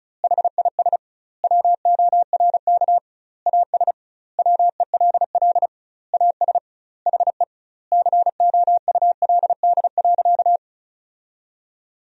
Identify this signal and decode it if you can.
Morse code, heard as a keyed tone, about 36 wpm, 700 Hz. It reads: HIS WORK AS WELL AS HE COULD.
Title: 2x Word-Spacing - Sentences from Top 500 Words - 35wpm